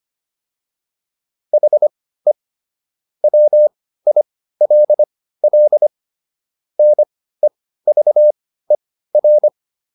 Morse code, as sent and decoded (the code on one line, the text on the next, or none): .... / .-- .. .-.. .-.. / -. ...- . .-.
HE WILL NEVER